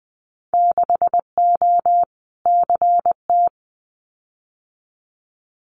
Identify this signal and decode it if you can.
Morse code, heard as a keyed tone, about 20 wpm, 700 Hz. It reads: 6O CT
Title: Winter Field Day Exchanges Encore 20wpm